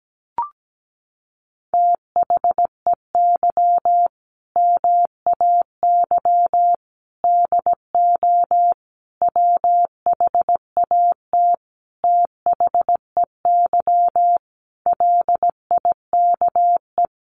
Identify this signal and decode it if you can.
Morse code, heard as a keyed tone, about 17 wpm, 700 Hz. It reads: THEY MAY DO WHAT THEY LIKE
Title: Sentences from Top 1200 Words 17wpm